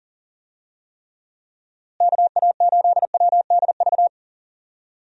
Morse code, heard as a keyed tone, about 40 wpm, 700 Hz. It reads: KA8WDV
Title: USA Call Signs - 6 Characters - Repeated 3x - 40wpm